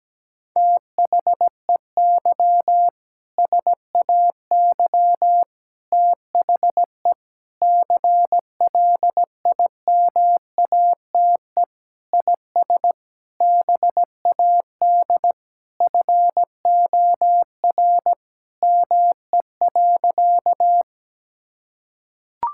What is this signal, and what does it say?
Morse code, heard as a keyed tone, about 17 wpm, 700 Hz. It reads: THEY SAY THE CLIMATE IS BAD FOR ME.
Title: Sentences from Top 1600 Words 17wpm